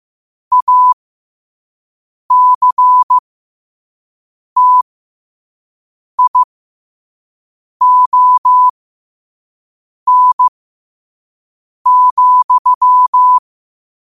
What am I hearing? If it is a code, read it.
Morse code, heard as a keyed tone, about 15 wpm, spaced as for 5 wpm, 1000 Hz. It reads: ACTION,
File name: qotdfiles / 5wpm